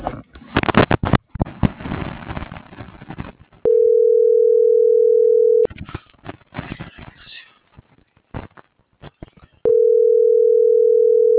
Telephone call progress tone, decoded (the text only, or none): ringback tone